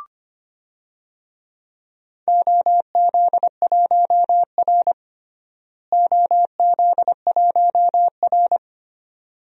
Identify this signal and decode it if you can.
Morse code, heard as a keyed tone, about 25 wpm, 700 Hz. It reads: OZ1R OZ1R
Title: DXCC-100 Call Signs - 4 Characters - Repeated 3x - 25wpm